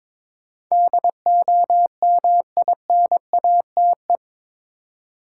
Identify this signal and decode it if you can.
Morse code, heard as a keyed tone, about 22 wpm, 700 Hz. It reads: DOMINATE
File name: practice / 22wpm / Eight Letter Words - 22wpm